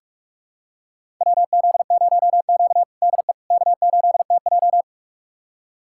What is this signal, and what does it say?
Morse code, heard as a keyed tone, about 46 wpm, 700 Hz. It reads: WZ0Q DE K8TJ
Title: USA Callsign DE Callsign - 4 Characters - 45wpm